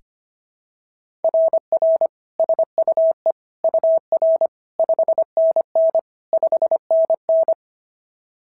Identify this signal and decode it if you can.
Morse code, heard as a keyed tone, about 24 wpm, 650 Hz. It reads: RR SUE UR 5NN 5NN
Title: SOTA QSO Practice 25wpm